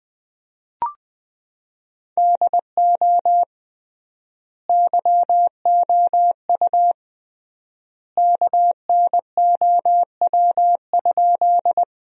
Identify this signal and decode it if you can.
Morse code, heard as a keyed tone, about 20 wpm, 700 Hz. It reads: DO YOU KNOW?
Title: 3x Word-Spacing - Sentences from Top 100 Words - 20wpm